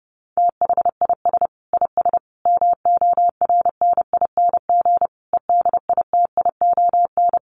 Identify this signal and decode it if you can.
Morse code, heard as a keyed tone, about 30 wpm, 700 Hz. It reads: THIS IS MORNING EDITION
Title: Sets of 5 Words Encore2 30wpm